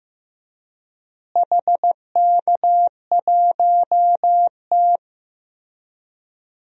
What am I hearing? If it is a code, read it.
Morse code, heard as a keyed tone, about 15 wpm, 700 Hz. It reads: HK1T
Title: DXCC-100 Call Signs - 4 Characters - Repeated 3x - 15wpm